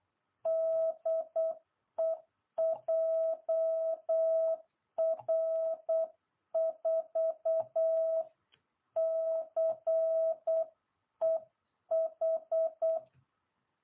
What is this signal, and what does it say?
Morse code, heard as a keyed tone, about 8 wpm, 660 Hz. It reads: DE JR4CEH